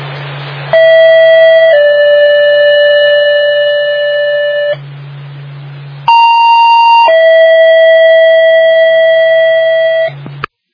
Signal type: two-tone page